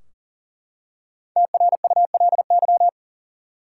Morse code, heard as a keyed tone, about 40 wpm, 700 Hz. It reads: TRULY
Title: Five Letter Words - 40wpm